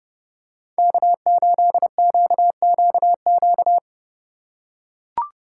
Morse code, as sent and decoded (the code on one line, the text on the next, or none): -.- ---.. --.- --.- --.-
K8QQQ